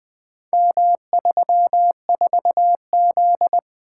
Morse code, heard as a keyed tone, about 20 wpm, 700 Hz. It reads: M34Z
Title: Four Letter-Number Combinations - 20wpm